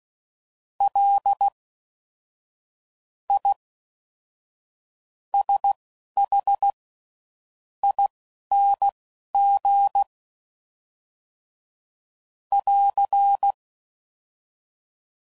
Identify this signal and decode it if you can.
Morse code, heard as a keyed tone, about 16 wpm, spaced as for 7 wpm, 800 Hz. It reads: LISHING <AR>